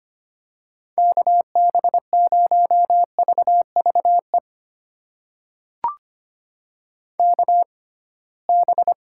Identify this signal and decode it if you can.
Morse code, heard as a keyed tone, about 25 wpm, spaced as for 19 wpm, 700 Hz. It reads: KB0VVE KB
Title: USA Call Signs - 6 Characters - 6x Character Spacing - 25wpm